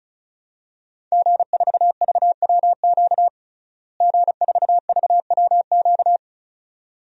Morse code, heard as a keyed tone, about 36 wpm, 700 Hz. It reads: G4VWQ G4VWQ
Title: DXCC-100 Call Signs - 5 Characters - Repeated 2x - 35wpm